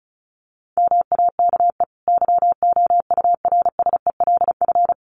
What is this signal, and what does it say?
Morse code, heard as a keyed tone, about 34 wpm, 700 Hz. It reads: MAKE YOURSELF
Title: Sentences from Top 600 Words 35wpm